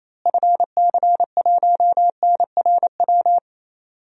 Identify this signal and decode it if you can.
Morse code, heard as a keyed tone, about 28 wpm, 700 Hz. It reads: FC1NRW
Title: DXCC-100 Call Signs - 6 Characters - Repeated 3x - 28wpm